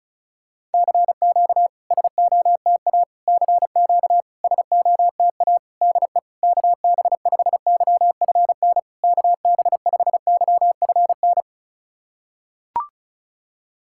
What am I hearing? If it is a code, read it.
Morse code, heard as a keyed tone, about 36 wpm, 690 Hz. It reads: CQ SOTA CQ SOTA DE KB5YFN KB5YFN
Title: SOTA QSO Practice 35wpm